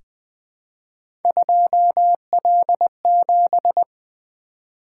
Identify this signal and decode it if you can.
Morse code, heard as a keyed tone, about 20 wpm, 700 Hz. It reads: 2L7
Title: Three Letter-Number Combinations - 20wpm